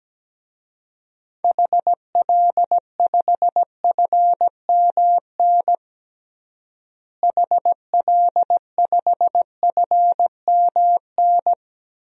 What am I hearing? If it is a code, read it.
Morse code, heard as a keyed tone, about 17 wpm, 700 Hz. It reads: HL5FMN HL5FMN
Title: DXCC-100 Call Signs - 6 Characters - Repeated 2x - 17wpm